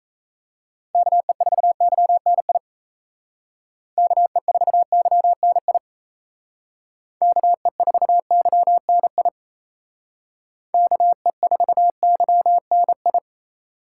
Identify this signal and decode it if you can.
Morse code, heard as a keyed tone, about 34 wpm, 700 Hz. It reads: KE4YNI KE4YNI KE4YNI KE4YNI
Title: Speed Racing - USA 2x3 Call Signs 28wpm